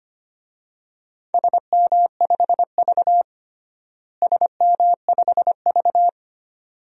Morse code, heard as a keyed tone, about 25 wpm, 700 Hz. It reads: SM5V SM5V
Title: DXCC-100 Call Signs - 4 Characters - Repeated 2x - 25wpm